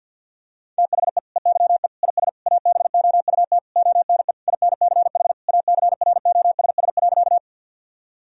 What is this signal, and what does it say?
Morse code, heard as a keyed tone, about 50 wpm, 700 Hz. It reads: THE EYE IS ABOUT ONE INCH ACROSS.
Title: Sentences from Top 1000 Words - Review Part 3 - 50wpm